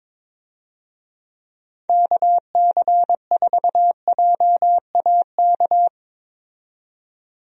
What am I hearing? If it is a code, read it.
Morse code, heard as a keyed tone, about 22 wpm, 700 Hz. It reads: KC4JAK
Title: USA Call Signs - 6 Characters - 22wpm